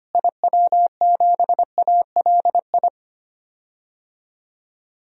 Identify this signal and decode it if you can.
Morse code, heard as a keyed tone, about 25 wpm, 700 Hz. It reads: IW7ALI
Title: DXCC-100 Call Signs - 6 Characters - Repeated 3x - 25wpm